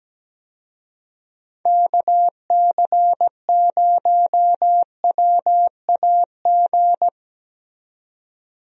Morse code, heard as a keyed tone, about 17 wpm, 700 Hz. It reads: KC0WAG
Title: USA Call Signs - 6 Characters - 17wpm